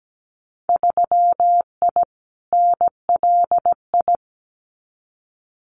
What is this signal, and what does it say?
Morse code, heard as a keyed tone, about 17 wpm, 700 Hz. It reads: 3I NLI
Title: Winter Field Day Exchanges Encore 17wpm